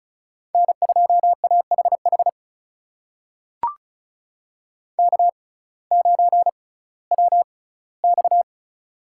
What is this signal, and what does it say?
Morse code, heard as a keyed tone, about 34 wpm, spaced as for 19 wpm, 700 Hz. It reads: N2AHH K9WX
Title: USA Call Signs - 6x Character Spacing - 35wpm